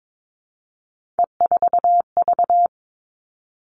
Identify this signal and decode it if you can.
Morse code, heard as a keyed tone, about 22 wpm, 700 Hz. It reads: E4V
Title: All Cut Numbers - 3-Digit 22wpm